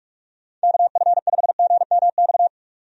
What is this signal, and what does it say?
Morse code, heard as a keyed tone, about 47 wpm, 700 Hz. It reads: KF5GMX
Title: USA Call Signs - 6 Characters - 45wpm